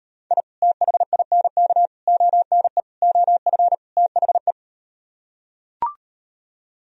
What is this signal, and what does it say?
Morse code, heard as a keyed tone, about 38 wpm, 700 Hz. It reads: I THINK ONE OF THE